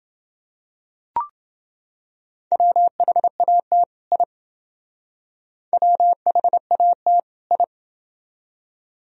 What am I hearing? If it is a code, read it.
Morse code, heard as a keyed tone, about 26 wpm, 700 Hz. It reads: WHAT I WHAT I